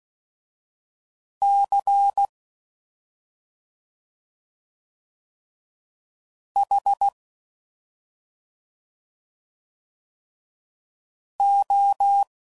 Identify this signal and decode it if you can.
Morse code, heard as a keyed tone, about 16 wpm, 800 Hz. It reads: C H O
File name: Suffixes / Special ING's / Sngl ING's